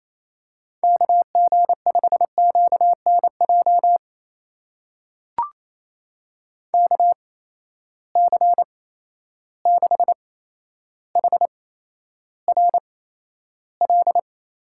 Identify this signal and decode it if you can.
Morse code, heard as a keyed tone, about 28 wpm, spaced as for 12 wpm, 700 Hz. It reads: KG5QNJ KC6HRL
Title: USA Call Signs - 6 Characters - 8x Character Spacing - 28wpm